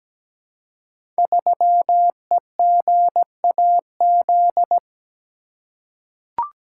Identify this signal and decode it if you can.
Morse code, heard as a keyed tone, about 17 wpm, 700 Hz. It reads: 3EGAZ
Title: Code Groups - 5 Letter:Numbers - 17wpm